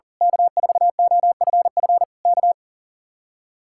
Morse code, heard as a keyed tone, about 38 wpm, 700 Hz. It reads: K4OFF K
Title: Speed-Racing Common QSO Phrases - 40wpm